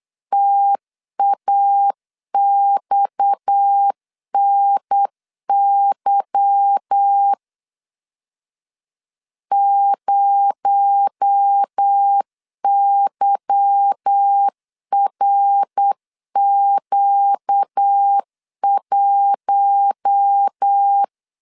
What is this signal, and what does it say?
Morse code, heard as a keyed tone, about 8 wpm, 800 Hz. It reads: TAXNY 0YRQ1
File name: LEZIONI / 8 YJ45 / 8 Y 8wpm 3245